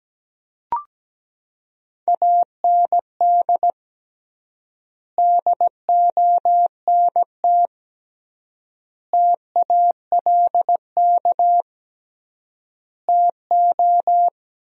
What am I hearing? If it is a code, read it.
Morse code, heard as a keyed tone, about 17 wpm, 700 Hz. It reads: AND DONT TALK TO
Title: 3x Word-Spacing - Sentences from Top 400 Words - 17wpm